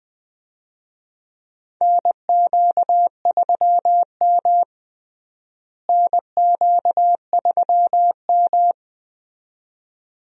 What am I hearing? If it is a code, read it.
Morse code, heard as a keyed tone, about 20 wpm, 700 Hz. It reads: NQ3M NQ3M
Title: USA Call Signs - 4 Characters - Repeated 2x - 20wpm